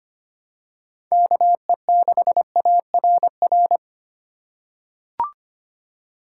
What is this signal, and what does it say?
Morse code, heard as a keyed tone, about 25 wpm, 700 Hz. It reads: KE6ARR
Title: USA Call Signs - 6 Characters - 3x Character Spacing - 25wpm